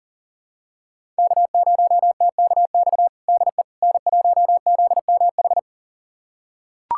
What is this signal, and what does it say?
Morse code, heard as a keyed tone, about 40 wpm, 700 Hz. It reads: K0TKX DE N1ZMH
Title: USA Callsign DE Callsign - 5 Characters - 40wpm